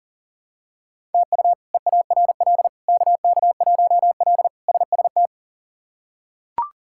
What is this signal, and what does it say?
Morse code, heard as a keyed tone, about 40 wpm, 700 Hz. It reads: TU EARL KK1L SST
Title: K1USN SST QSO Practice - 40wpm